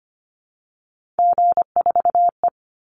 Morse code, heard as a keyed tone, about 25 wpm, 700 Hz. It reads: G4E
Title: Call Signs Repeated 2x - 25wpm